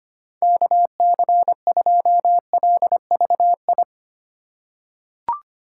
Morse code, heard as a keyed tone, about 25 wpm, 700 Hz. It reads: KC2LVI
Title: USA Call Signs - 6 Characters - 8x Character Spacing - 25wpm